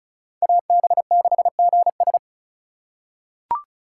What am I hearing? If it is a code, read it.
Morse code, heard as a keyed tone, about 36 wpm, 700 Hz. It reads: AB6GS
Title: USA Call Signs - 5 Characters - 8x Character Spacing - 35wpm